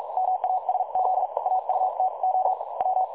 Morse code, heard as a keyed tone, about 46 wpm, spaced as for 39 wpm, 750 Hz, with noise. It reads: MASZ AUTO W
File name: dlazaawansowanych / nagraniazpasma / websdr recording 3565.0kHz